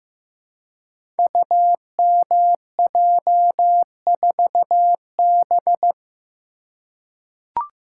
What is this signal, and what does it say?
Morse code, heard as a keyed tone, about 15 wpm, 700 Hz. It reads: UMJ4B
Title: Code Groups - 5 Letter:Numbers - 15wpm